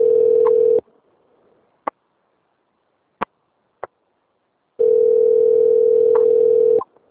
Ringback tone (North America), 440 + 480 Hz.
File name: out- - - - - .58473